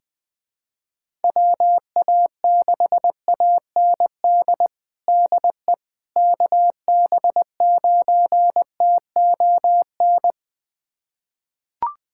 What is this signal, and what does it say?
Morse code, heard as a keyed tone, about 20 wpm, 700 Hz. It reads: WA6AND DE KB9TON
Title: USA Callsign DE Callsign 20wpm